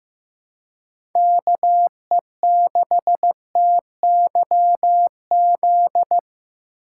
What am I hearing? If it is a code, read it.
Morse code, heard as a keyed tone, about 18 wpm, 700 Hz. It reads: KE6TYZ
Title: Speed Racing - USA Call Signs 15wpm